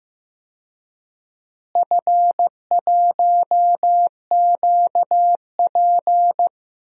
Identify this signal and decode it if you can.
Morse code, heard as a keyed tone, about 15 wpm, 700 Hz. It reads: F1QP
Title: DXCC-100 Call Signs - 4 Characters - Repeated 2x - 15wpm